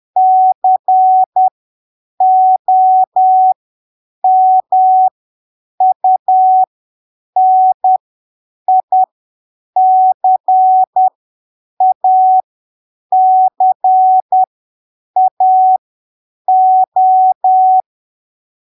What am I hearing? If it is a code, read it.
Morse code, heard as a keyed tone, about 10 wpm, spaced as for 7 wpm, 750 Hz. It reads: COMUNICACAO
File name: arquivos / texto / texto 04 10ppm